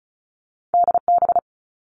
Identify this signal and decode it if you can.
Morse code, heard as a keyed tone, about 36 wpm, 700 Hz. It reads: DB
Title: All Cut Numbers - 2-Digit 35wpm